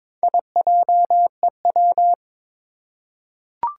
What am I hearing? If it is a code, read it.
Morse code, heard as a keyed tone, about 22 wpm, 700 Hz. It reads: IJEW
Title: Four Challenging Letter-Number Combinations 22wpm